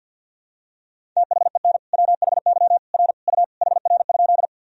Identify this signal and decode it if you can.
Morse code, heard as a keyed tone, about 52 wpm, 700 Hz. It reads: THEN WHY R U HR?
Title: Sentences from Top 200 Words and Abbreviations 50wpm